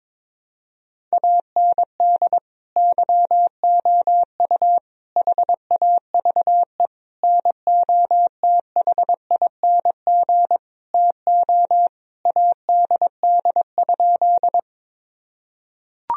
Morse code, heard as a keyed tone, about 22 wpm, 700 Hz. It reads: AND YOU HAVE NOTHING TO ADD?
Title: Sentences from Top 1000 Words - Review Part 1 - 22wpm